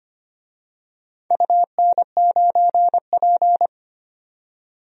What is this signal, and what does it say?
Morse code, heard as a keyed tone, about 25 wpm, 700 Hz. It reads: UN9P
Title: DXCC-100 Call Signs - 4 Characters - 25wpm